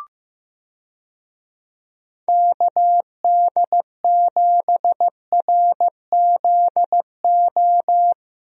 Morse code, heard as a keyed tone, about 16 wpm, 700 Hz. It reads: KD7RZO
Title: USA Call Signs - 6 Characters - Repeated 3x - 15wpm